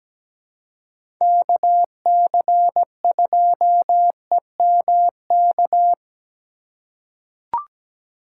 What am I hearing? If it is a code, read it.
Morse code, heard as a keyed tone, about 17 wpm, 700 Hz. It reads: KC2EMK
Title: USA Call Signs - 6 Characters - 2x Character Spacing - 17wpm